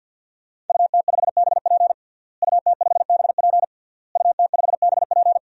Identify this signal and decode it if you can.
Morse code, heard as a keyed tone, about 52 wpm, 700 Hz. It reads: UT5BP UT5BP UT5BP
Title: DXCC-100 Call Signs - 5 Characters - Repeated 3x - 50wpm